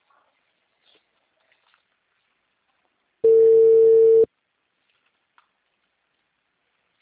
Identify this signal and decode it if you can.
Ringback tone (China, 450 Hz).